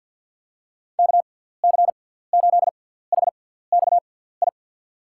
Morse code, heard as a keyed tone, about 47 wpm, spaced as for 17 wpm, 700 Hz. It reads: KC8HXI
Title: USA Call Signs - 6 Characters - 6x Character Spacing - 50wpm